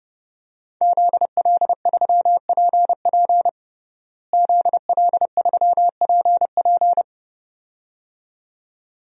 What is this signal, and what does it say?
Morse code, heard as a keyed tone, about 32 wpm, 700 Hz. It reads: ZL3PP ZL3PP
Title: DXCC-100 Call Signs - 5 Characters - Repeated 2x - 30wpm